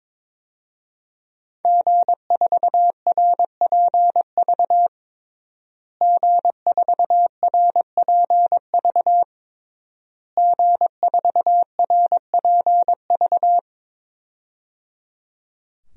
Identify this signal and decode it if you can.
Morse code, heard as a keyed tone, about 22 wpm, 700 Hz. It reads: G4RPV G4RPV G4RPV
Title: DXCC-100 Call Signs - 5 Characters - Repeated 3x - 22wpm